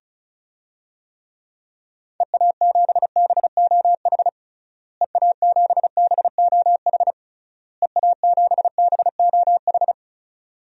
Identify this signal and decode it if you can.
Morse code, heard as a keyed tone, about 36 wpm, 700 Hz. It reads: EA7BOH EA7BOH EA7BOH
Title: DXCC-100 Call Signs - 6 Characters - Repeated 3x - 35wpm